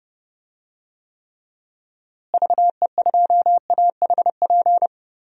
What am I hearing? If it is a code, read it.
Morse code, heard as a keyed tone, about 30 wpm, 700 Hz. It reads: VE2AHP
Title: DXCC-100 Call Signs - 6 Characters - Repeated 2x - 30wpm